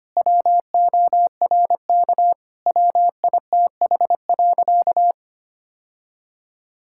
Morse code, heard as a keyed tone, about 25 wpm, 700 Hz. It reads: WORK WITH.
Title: Sentences from Top 1000 Words - Review Part 5 - 25wpm